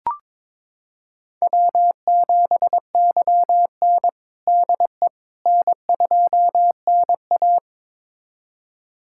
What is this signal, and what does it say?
Morse code, heard as a keyed tone, about 21 wpm, 700 Hz. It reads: W7YN DE N2NA
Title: USA Callsign DE Callsign - 4 Characters - 22wpm